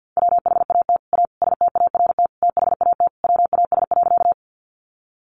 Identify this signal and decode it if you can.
Morse code, heard as a keyed tone, about 50 wpm, 700 Hz. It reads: WHAT A START THAT WAS.
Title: Sentences from Top 300 Words 50wpm